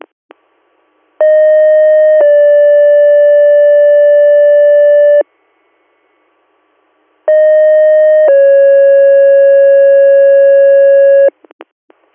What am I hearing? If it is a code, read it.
Two-tone page, heard as a one-second tone, then a three-second tone, about 620 then 580 Hz, about 620 then 550 Hz. It is heard from 1.2 to 11.3 s.